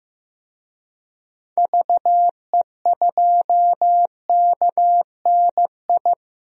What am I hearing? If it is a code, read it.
Morse code, heard as a keyed tone, about 15 wpm, 700 Hz. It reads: VE2KNI
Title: DXCC-100 Call Signs - 6 Characters - 15wpm